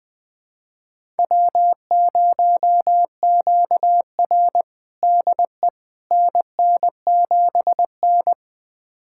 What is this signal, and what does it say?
Morse code, heard as a keyed tone, about 20 wpm, 700 Hz. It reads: W0QR DE NN7N
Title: USA Callsign DE Callsign - 4 Characters - 20wpm